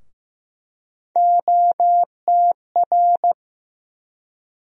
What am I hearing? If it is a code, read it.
Morse code, heard as a keyed tone, about 16 wpm, 700 Hz. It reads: OTR
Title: USA Call Sign Postfixes - 3 Letters 15wpm